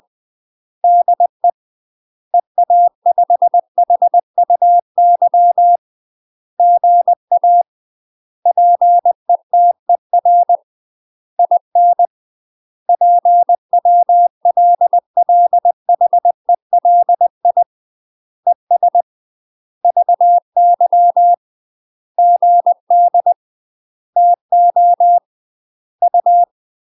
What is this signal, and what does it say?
Morse code, heard as a keyed tone, about 20 wpm, 700 Hz. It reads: DE EA5HUY GA PETER IN PWLLHELI ES VY GD TO U
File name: UK MASTER QSOs TRAINING / 20WPM / UK MASTER QSOs 0043